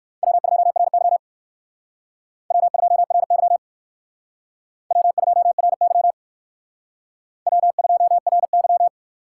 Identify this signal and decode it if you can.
Morse code, heard as a keyed tone, about 56 wpm, 700 Hz. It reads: W2RY W2RY W2RY W2RY